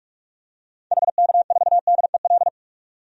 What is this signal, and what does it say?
Morse code, heard as a keyed tone, about 46 wpm, 700 Hz. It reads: HK4DEL
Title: DXCC-100 Call Signs - 6 Characters - 45wpm